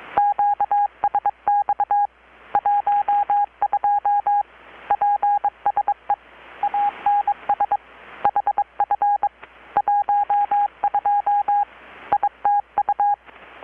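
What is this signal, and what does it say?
Morse code, heard as a keyed tone, about 22 wpm, 820 Hz. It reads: QSX 12 PSE ANS HF 12 ITU
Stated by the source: Morse from KPH's Night of Nights XX maritime radiotelegraphy event, received in Chiapas, Mexico about 3,800 km away on a Kenwood R-600 with a grounded 10 m long-wire antenna.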